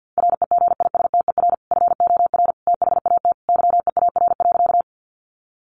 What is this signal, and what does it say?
Morse code, heard as a keyed tone, about 50 wpm, 700 Hz. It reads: REGISTER FOR THAT YEAR.
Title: Sentences from Top 1200 Words 50wpm